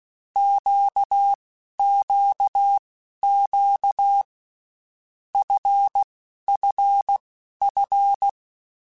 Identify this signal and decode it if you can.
Morse code, heard as a keyed tone, about 16 wpm, spaced as for 11 wpm, 800 Hz. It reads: QQQ FFF
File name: Groups@10 / Groups@10 mp3s / LFCYQ@10